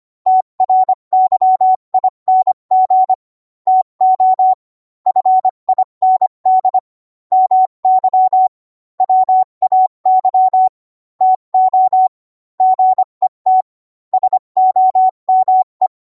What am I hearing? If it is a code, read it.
Morse code, heard as a keyed tone, about 25 wpm, 750 Hz. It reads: TRYING TO FIND MY WAY TO GET SOME